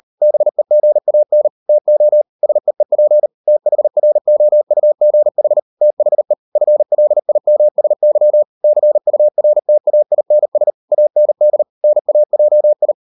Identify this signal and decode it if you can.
Morse code, heard as a keyed tone, about 38 wpm, 600 Hz. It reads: BEGAN TO SEEP THROUGH THE FLIMSY CURTAINS AND NAJI